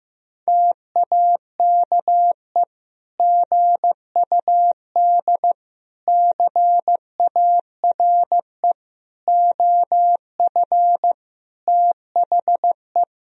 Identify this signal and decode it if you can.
Morse code, heard as a keyed tone, about 15 wpm, 700 Hz. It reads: TAKE GUD CARE OF THE
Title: Sentences from Top 400 Words and Abbreviations 15wpm